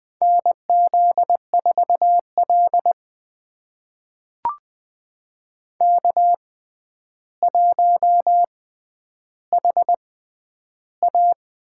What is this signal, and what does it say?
Morse code, heard as a keyed tone, about 21 wpm, spaced as for 10 wpm, 700 Hz. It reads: NZ4L K1HA
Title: USA Call Signs - 4 Characters - 6x Character Spacing - 20wpm